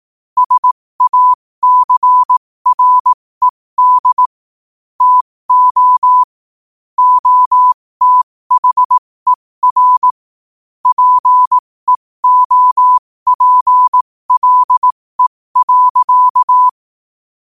Morse code, heard as a keyed tone, about 18 wpm, spaced as for 15 wpm, 1000 Hz. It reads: SACRED TO OTHER PEOPLE.
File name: 15wpm